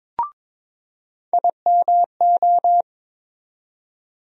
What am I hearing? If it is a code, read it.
Morse code, heard as a keyed tone, about 21 wpm, 700 Hz. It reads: IMO
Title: USA Call Sign Postfixes - 3 Letters 22wpm